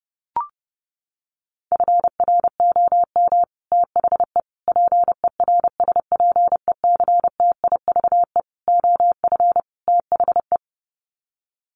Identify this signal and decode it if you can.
Morse code, heard as a keyed tone, about 30 wpm, 700 Hz. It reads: FROM THE PERSPECTIVE OF THE
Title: Sets of 5 Words Encore2 30wpm